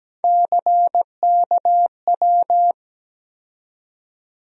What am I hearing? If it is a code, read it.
Morse code, heard as a keyed tone, about 17 wpm, 700 Hz. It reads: CKW
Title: USA Call Sign Postfixes - 3 Letters 17wpm